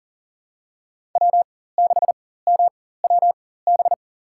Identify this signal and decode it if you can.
Morse code, heard as a keyed tone, about 40 wpm, spaced as for 19 wpm, 700 Hz. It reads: W6MWB
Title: USA Call Signs - 5 Characters - 4x Character Spacing - 40wpm